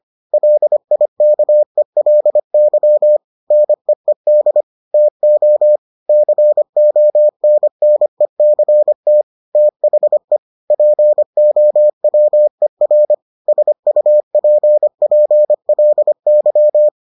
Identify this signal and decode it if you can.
Morse code, heard as a keyed tone, about 25 wpm, 600 Hz. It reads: LIKELY NEED TO CONNECT THE POWER SUPPLY